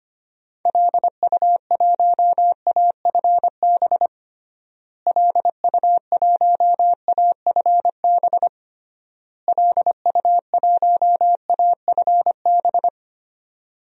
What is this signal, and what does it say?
Morse code, heard as a keyed tone, about 25 wpm, 700 Hz. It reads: LU1AFB LU1AFB LU1AFB
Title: DXCC-100 Call Signs - 6 Characters - Repeated 3x - 25wpm